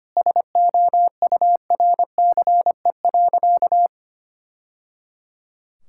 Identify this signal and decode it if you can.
Morse code, heard as a keyed tone, about 25 wpm, 700 Hz. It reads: SOURCE.
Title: Sentences from Top 1000 Words - Review Part 5 - 25wpm